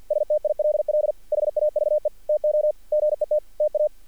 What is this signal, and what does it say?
Morse code, heard as a keyed tone, about 48 wpm, 610 Hz. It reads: STILL HAVE TO GET TA
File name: archived / CRSnet